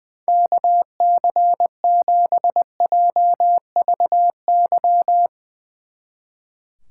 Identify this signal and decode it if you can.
Morse code, heard as a keyed tone, about 20 wpm, 700 Hz. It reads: KC7JVY
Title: Speed Racing - USA 2x3 Call Signs 20wpm